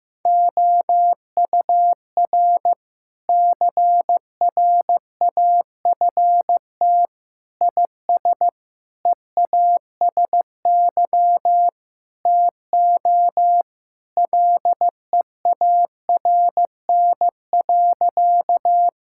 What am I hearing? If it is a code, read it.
Morse code, heard as a keyed tone, about 15 wpm, 700 Hz. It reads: OUR CRAFT IS EASY TO LEARN.
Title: Sentences from Top 1000 Words - Review Part 6 - 15wpm